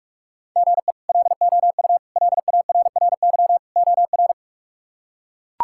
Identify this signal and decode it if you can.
Morse code, heard as a keyed tone, about 42 wpm, 700 Hz. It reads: GE LOU LARRY OR
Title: K1USN SST QSO Practice - 45wpm